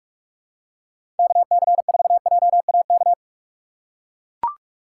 Morse code, heard as a keyed tone, about 43 wpm, 700 Hz. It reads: KC4JAK
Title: USA Call Signs - 4x Character Spacing - 45wpm